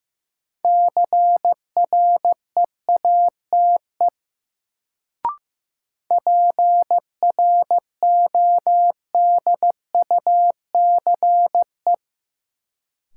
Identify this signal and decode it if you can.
Morse code, heard as a keyed tone, about 15 wpm, 700 Hz. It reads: CREATE PRODUCE
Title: Sentences from Top 1000 Words - Review Part 3 - New Word Drill - 15wpm